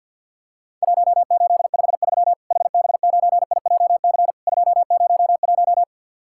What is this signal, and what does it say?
Morse code, heard as a keyed tone, about 50 wpm, 700 Hz. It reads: 1853 HB9IJC 201
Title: Worked All Europe DX Contest - QTC Exchange with Repeats - 50wpm